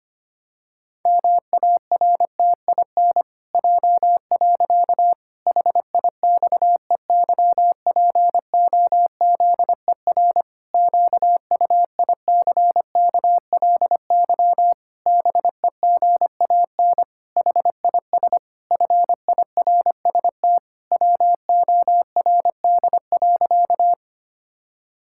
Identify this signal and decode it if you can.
Morse code, heard as a keyed tone, about 25 wpm, 700 Hz. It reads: MARTIN J. HIXEYPOZER QUICKLY BEGAN HIS FIRST WORD.
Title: Pangram Challenge v2 - 25wpm